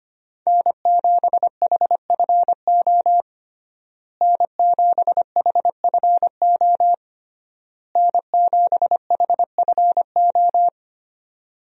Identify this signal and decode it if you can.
Morse code, heard as a keyed tone, about 25 wpm, 700 Hz. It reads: N7HFO N7HFO N7HFO
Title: USA Call Signs - 5 Characters - Repeated 3x - 25wpm